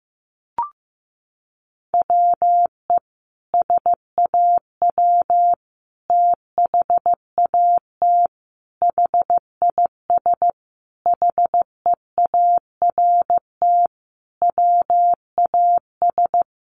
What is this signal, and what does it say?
Morse code, heard as a keyed tone, about 15 wpm, 700 Hz. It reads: WE SAW THAT HIS HEART WAS
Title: Sentences from Top 1200 Words 15wpm